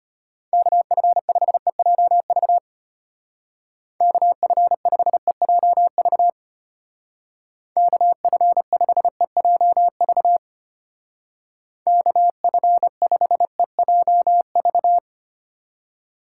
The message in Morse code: KF5EJV KF5EJV KF5EJV KF5EJV

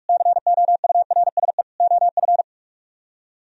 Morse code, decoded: COURSE OF

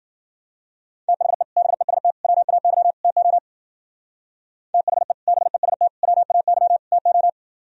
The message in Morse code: THE BEST WAY TO THE BEST WAY TO

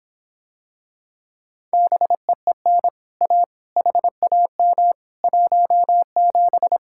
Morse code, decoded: BEEN A HAM 17

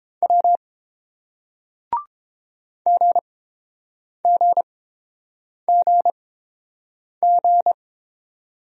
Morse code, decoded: W GGGG

